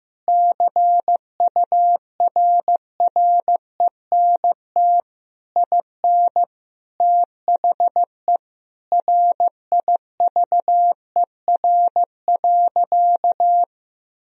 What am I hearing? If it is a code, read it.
Morse code, heard as a keyed tone, about 15 wpm, 700 Hz. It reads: CURRENT IN THE RIVER.
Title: Sentences from Top 1000 Words - Review Part 4 - 15wpm